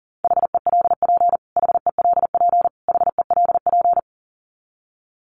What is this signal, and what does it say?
Morse code, heard as a keyed tone, about 40 wpm, 700 Hz. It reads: HELP HELP HELP
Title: Top 500 Words Repeated 40wpm